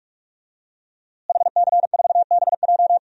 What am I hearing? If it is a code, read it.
Morse code, heard as a keyed tone, about 46 wpm, 700 Hz. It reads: HC4BJ
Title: DXCC-100 Call Signs - 5 Characters - Repeated 2x - 45wpm